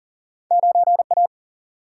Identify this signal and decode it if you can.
Morse code, heard as a keyed tone, about 40 wpm, 700 Hz. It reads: 9 A
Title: Single-Letter Single-Number 40wpm